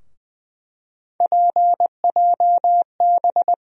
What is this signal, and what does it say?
Morse code, heard as a keyed tone, about 20 wpm, 700 Hz. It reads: PJB
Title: USA Call Sign Postfixes - 3 Letters 20wpm